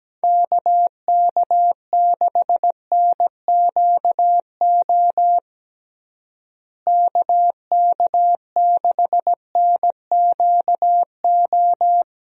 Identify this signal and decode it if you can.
Morse code, heard as a keyed tone, about 17 wpm, 700 Hz. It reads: KK6NQO KK6NQO
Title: USA Call Signs - 6 Characters - Repeated 2x - 17wpm